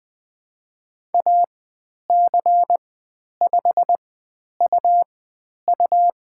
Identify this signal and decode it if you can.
Morse code, heard as a keyed tone, about 20 wpm, spaced as for 10 wpm, 700 Hz. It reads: AC5UU
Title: 109 Lesson 18g - Letter m - Call Signs 20wpm